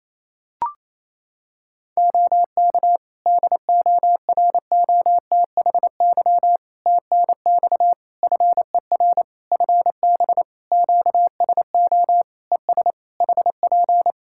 Morse code, heard as a keyed tone, about 27 wpm, 700 Hz. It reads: OK DOROTHY TNX FER FB QSO ES HP
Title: Practice QSO - Exchange 4 - Ending - 28wpm